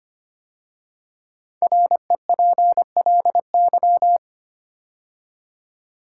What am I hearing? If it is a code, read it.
Morse code, heard as a keyed tone, about 25 wpm, 700 Hz. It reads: REPLY
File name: 5 Letter Words - First 3 Letters Given - 25wpm